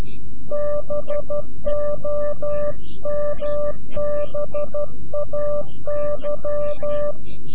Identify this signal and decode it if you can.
Morse code, heard as a keyed tone, about 12 wpm, 590 Hz. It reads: BOMBAY
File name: CW-QSO / CQ Call / CQ-QSO